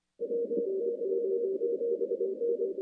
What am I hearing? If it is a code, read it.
Morse code, heard as a keyed tone, about 24 wpm, 470 Hz. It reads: WY7N
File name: contests / CW-SS-k5tr- -2200